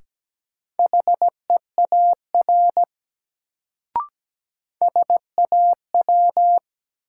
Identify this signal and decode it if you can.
Morse code, heard as a keyed tone, about 17 wpm, 700 Hz. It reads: HEAR SAW
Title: Sentences from Top 300 Words - New Word Drill - 17wpm